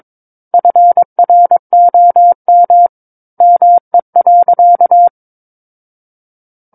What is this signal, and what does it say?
Morse code, heard as a keyed tone, about 22 wpm, 700 Hz. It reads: FROM ME.
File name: Book One / 13-Chapter 13 - 22 WPM